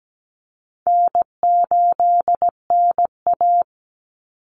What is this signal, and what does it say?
Morse code, heard as a keyed tone, about 17 wpm, 700 Hz. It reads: N8NA
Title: Easy Call Signs - 17wpm